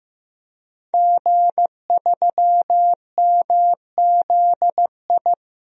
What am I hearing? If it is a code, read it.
Morse code, heard as a keyed tone, about 16 wpm, 700 Hz. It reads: G3MZI
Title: DXCC-100 Call Signs - 5 Characters - Repeated 3x - 15wpm